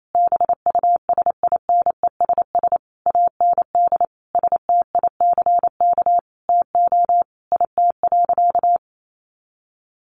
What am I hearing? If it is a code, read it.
Morse code, heard as a keyed tone, about 28 wpm, 700 Hz. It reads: BUSINESS AND STICK TO IT.
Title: Sentences from Top 1200 Words 28wpm